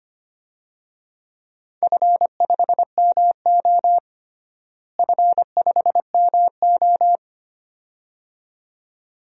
Morse code, heard as a keyed tone, about 25 wpm, 700 Hz. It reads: F5MO F5MO
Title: DXCC-100 Call Signs - 4 Characters - Repeated 2x - 25wpm